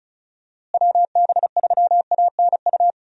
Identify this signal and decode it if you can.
Morse code, heard as a keyed tone, about 36 wpm, 700 Hz. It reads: WB3ANU
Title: USA Call Signs - 6 Characters - Repeated 2x - 35wpm